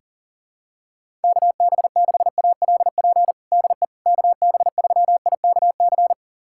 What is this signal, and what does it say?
Morse code, heard as a keyed tone, about 37 wpm, 700 Hz. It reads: KB6ALP DE KB3IKC
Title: USA Callsign DE Callsign 40wpm